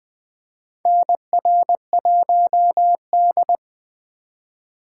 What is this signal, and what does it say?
Morse code, heard as a keyed tone, about 20 wpm, 700 Hz. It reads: NR1D
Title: USA Call Signs - 4 Characters - 3x Character Spacing - 20wpm